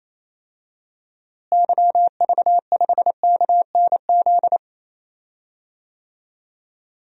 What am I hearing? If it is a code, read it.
Morse code, heard as a keyed tone, about 28 wpm, 700 Hz. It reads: YV5KNZ
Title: DXCC-100 Call Signs - 6 Characters - Repeated 3x - 28wpm